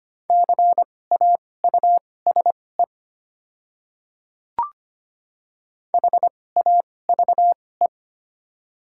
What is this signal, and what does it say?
Morse code, heard as a keyed tone, about 25 wpm, spaced as for 18 wpm, 700 Hz. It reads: CAUSE HAVE
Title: 2x Character Spacing - Top 500 Words - 25wpm